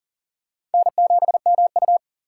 Morse code, heard as a keyed tone, about 40 wpm, 700 Hz. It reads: N7MU
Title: USA Call Signs - 4 Characters - 2x Character Spacing - 40wpm